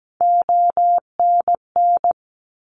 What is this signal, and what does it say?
Morse code, heard as a keyed tone, about 17 wpm, 700 Hz. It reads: ONN
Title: ARRL Sections 17wpm